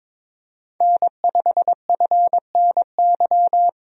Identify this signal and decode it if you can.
Morse code, heard as a keyed tone, about 22 wpm, 700 Hz. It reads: N5FNY
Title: USA Call Signs - 5 Characters - 22wpm